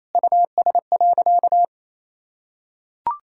Morse code, heard as a keyed tone, about 28 wpm, 700 Hz. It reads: US.
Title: Sentences from Top 1000 Words - Review Part 5 - 28wpm